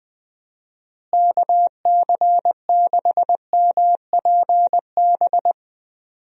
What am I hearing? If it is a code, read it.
Morse code, heard as a keyed tone, about 20 wpm, 700 Hz. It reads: KC6MPB
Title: USA Call Signs - 6 Characters - Repeated 3x - 20wpm